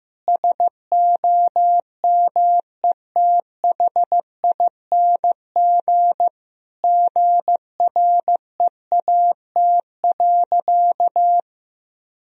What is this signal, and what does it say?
Morse code, heard as a keyed tone, about 15 wpm, 700 Hz. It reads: SOMETHING GREAT.
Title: Sentences from Top 1000 Words - Review Part 2 - 15wpm